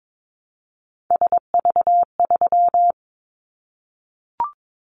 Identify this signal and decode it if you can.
Morse code, heard as a keyed tone, about 22 wpm, 700 Hz. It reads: SV3
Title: Callsign Prefixes for Top 50 DXCC - 22wpm